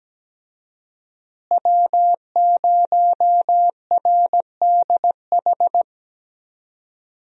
Morse code, heard as a keyed tone, about 17 wpm, 700 Hz. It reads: W0RDH
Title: USA Call Signs - 5 Characters - Repeated 2x - 17wpm